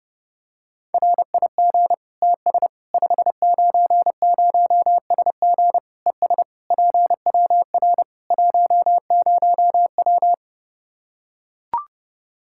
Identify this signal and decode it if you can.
Morse code, heard as a keyed tone, about 30 wpm, 700 Hz. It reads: RIG TS 590SG ES PWR 10W
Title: Practice QSO - Exchange 2 - Info - 30wpm